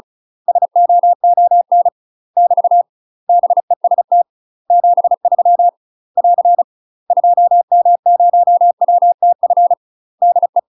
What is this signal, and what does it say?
Morse code, heard as a keyed tone, about 36 wpm, 700 Hz. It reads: SOON <BT> BEST 73 <AR> 2M0WTF DE
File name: UK MASTER QSOs TRAINING/35WPM